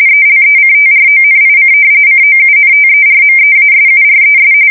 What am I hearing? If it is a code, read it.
RTTY, about 45.45 baud. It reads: ELECTRONIC COUNTERMEASURES
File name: fsk170 / battleshipnj